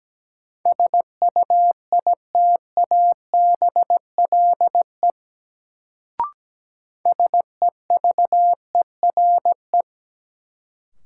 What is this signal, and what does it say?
Morse code, heard as a keyed tone, about 17 wpm, 700 Hz. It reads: SUITABLE SEVERE
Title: Sentences from Top 1400 Words - New Word Drill - 17wpm